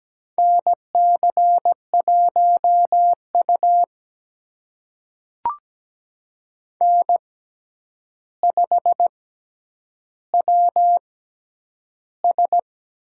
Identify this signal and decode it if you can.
Morse code, heard as a keyed tone, about 17 wpm, spaced as for 9 wpm, 700 Hz. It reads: NC1U N5WS
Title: USA Call Signs - 6x Character Spacing - 17wpm